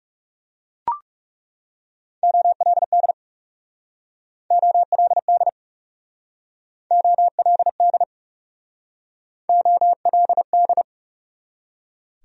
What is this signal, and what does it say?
Morse code, heard as a keyed tone, about 37 wpm, 700 Hz. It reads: OLD OLD OLD OLD
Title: Speed-Racing Top 500 Words 30wpm